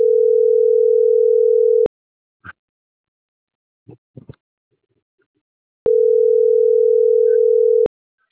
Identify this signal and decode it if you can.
Ringback tone (North America), 440 + 480 Hz.